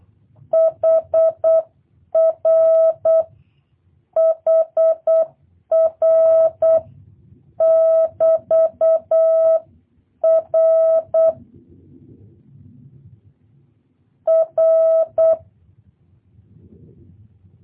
Morse code, heard as a keyed tone, about 8 wpm, 650 Hz. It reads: HR HR <BT> R R
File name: jh8cbh / 2023cw / review1-2